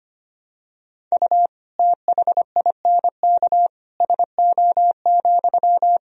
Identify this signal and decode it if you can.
Morse code, heard as a keyed tone, about 25 wpm, 700 Hz. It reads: U THINK SO,
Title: Sentences from Top 200 Words and Abbreviations 25wpm